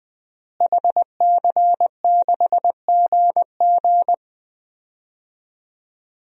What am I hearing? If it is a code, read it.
Morse code, heard as a keyed tone, about 21 wpm, 700 Hz. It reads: HC6GG
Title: DXCC-100 Call Signs - 5 Characters - Repeated 2x - 20wpm